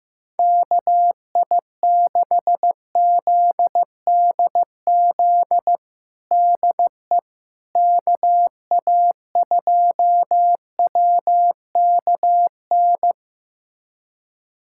Morse code, heard as a keyed tone, about 15 wpm, 700 Hz. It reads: KI6ZDZ DE KA2WKN
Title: USA Callsign DE Callsign 15wpm